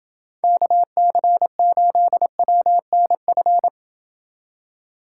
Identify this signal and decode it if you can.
Morse code, heard as a keyed tone, about 30 wpm, 700 Hz. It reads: KC8WNF